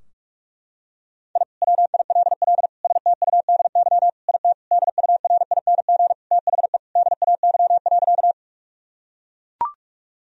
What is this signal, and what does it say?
Morse code, heard as a keyed tone, about 41 wpm, 700 Hz. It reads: I WILL STUDY IT DURING THE DAY.
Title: Sentences from Top 1000 Words - Review Part 2 - 45wpm